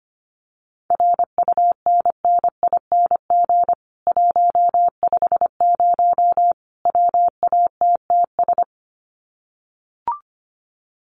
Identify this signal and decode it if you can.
Morse code, heard as a keyed tone, about 25 wpm, 700 Hz. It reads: RUNNING 150 WATTS
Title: Common QSO Phrases 25wpm